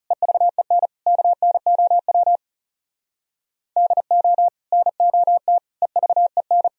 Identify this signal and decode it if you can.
Morse code, heard as a keyed tone, about 40 wpm, 700 Hz. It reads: EVEN KNOW DO NOT EVEN